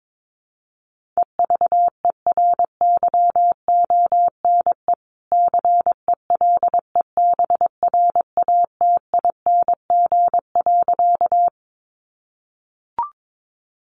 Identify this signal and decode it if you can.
Morse code, heard as a keyed tone, about 22 wpm, 700 Hz. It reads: EVERYONE CELEBRATING.
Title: A Sound Of Thunder 22wpm